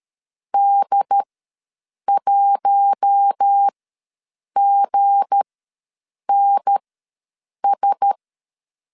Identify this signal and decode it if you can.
Morse code, heard as a keyed tone, about 13 wpm, spaced as for 7 wpm, 800 Hz. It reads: D1GNS
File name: LEZIONI / 7 GF9Z / 7 GF ws 12wpm 3140